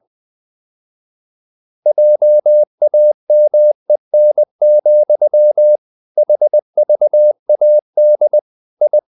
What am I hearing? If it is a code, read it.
Morse code, heard as a keyed tone, about 20 wpm, 600 Hz. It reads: JAMEN, HVAD I
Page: Ferie 20 wpm